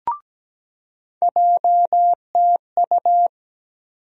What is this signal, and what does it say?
Morse code, heard as a keyed tone, about 17 wpm, 700 Hz. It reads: JTU
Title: USA Call Sign Postfixes - 3 Letters 17wpm